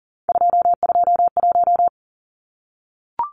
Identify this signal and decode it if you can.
Morse code, heard as a keyed tone, about 40 wpm, 700 Hz. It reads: 221
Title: Numbers - 3-Digit 40wpm